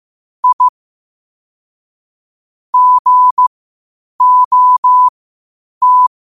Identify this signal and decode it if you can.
Morse code, heard as a keyed tone, about 15 wpm, spaced as for 8 wpm, 1000 Hz. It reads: I GOT
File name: qotdfiles / 8wpm